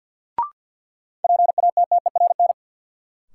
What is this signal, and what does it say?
Morse code, heard as a keyed tone, about 52 wpm, 700 Hz. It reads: PATTERN
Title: Sentences from Top 1000 Words - New Word Drill - 50wpm